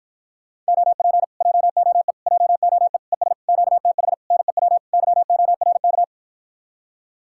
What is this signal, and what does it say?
Morse code, heard as a keyed tone, about 53 wpm, 700 Hz. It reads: OP JOE JOE ES QTH NEW YORK